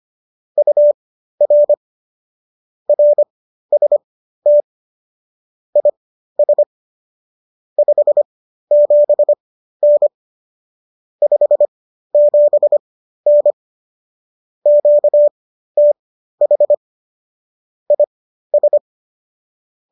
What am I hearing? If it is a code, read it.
Morse code, heard as a keyed tone, about 25 wpm, spaced as for 13 wpm, 590 Hz. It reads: UR RST IS 57N 57N QTH IS